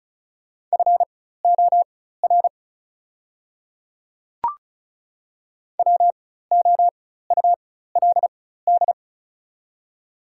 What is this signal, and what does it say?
Morse code, heard as a keyed tone, about 36 wpm, spaced as for 16 wpm, 690 Hz. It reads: FOR WOULD
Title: 4x Character Spacing - Top 200 Words - 35wpm